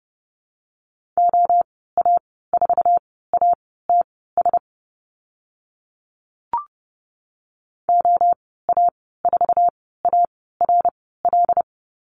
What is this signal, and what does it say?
Morse code, heard as a keyed tone, about 30 wpm, spaced as for 17 wpm, 700 Hz. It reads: OA4ATS OA4ARL
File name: Lesson 14g - Number 2 - Call Signs 30wpm